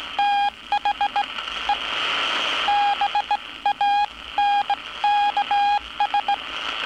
Morse code, heard as a keyed tone, about 17 wpm, spaced as for 15 wpm, 820 Hz. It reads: THE BANKS
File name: training / NAVPERS 11049to11060 / NAVPERS 11059 RA RB